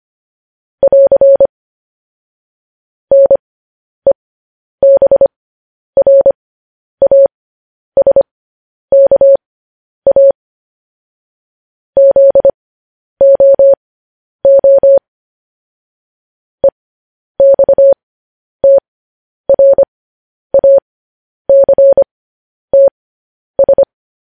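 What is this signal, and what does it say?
Morse code, heard as a keyed tone, about 25 wpm, spaced as for 10 wpm, 550 Hz. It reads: <AR> NEBRASKA ZOO EXTRACTS
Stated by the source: Morse code news headlines.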